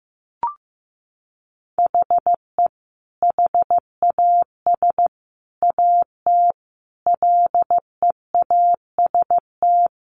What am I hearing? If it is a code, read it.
Morse code, heard as a keyed tone, about 15 wpm, 700 Hz. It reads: HE HAS AT LEAST
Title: Sentences from Top 1500 Words 15wpm